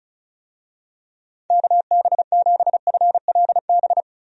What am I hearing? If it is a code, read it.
Morse code, heard as a keyed tone, about 36 wpm, 700 Hz. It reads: KB7FLB
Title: USA Call Signs - 6 Characters - 35wpm